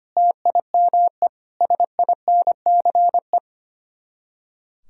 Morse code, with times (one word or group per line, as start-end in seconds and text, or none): 0.2-1.3 TIME
1.6-3.4 SINCE